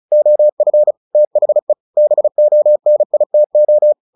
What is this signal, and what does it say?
Morse code, heard as a keyed tone, about 34 wpm, 600 Hz. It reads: OF THE BONITO
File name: the old man and the sea / the old man and the sea 1 35wpm